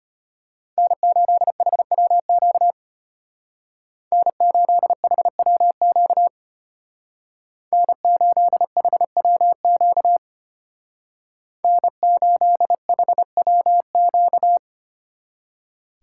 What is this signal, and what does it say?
Morse code, heard as a keyed tone, about 30 wpm, 700 Hz. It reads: N8HWQ N8HWQ N8HWQ N8HWQ
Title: Speed Racing - USA 1x3 Call Signs 25wpm